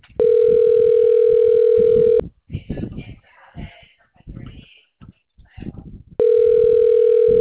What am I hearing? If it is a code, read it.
Ringback tone (North America), 440 + 480 Hz, with speech.
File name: out- -2014- - - .86006